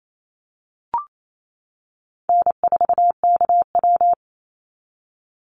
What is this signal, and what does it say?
Morse code, heard as a keyed tone, about 28 wpm, 700 Hz. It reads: N4KW
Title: Call Signs - 28wpm